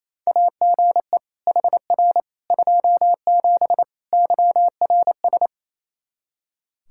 Morse code, heard as a keyed tone, about 28 wpm, 700 Hz. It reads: AGE HR 27 YRS
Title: Practice QSO - Exchange 3 - More Info - 28wpm